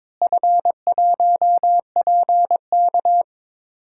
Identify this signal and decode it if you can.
Morse code, heard as a keyed tone, about 22 wpm, 700 Hz. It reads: F1PK
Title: DXCC-100 Call Signs - 4 Characters - Repeated 3x - 22wpm